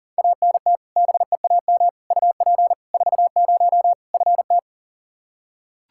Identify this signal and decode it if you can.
Morse code, heard as a keyed tone, about 40 wpm, 700 Hz. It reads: ANT BEAM UP 40 FT
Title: Practice QSO - Exchange 2 - Info - 40wpm